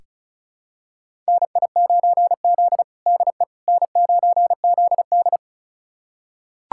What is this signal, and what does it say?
Morse code, heard as a keyed tone, about 34 wpm, 700 Hz. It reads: NI9Z DE N9ZD
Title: USA Callsign DE Callsign - 4 Characters - 35wpm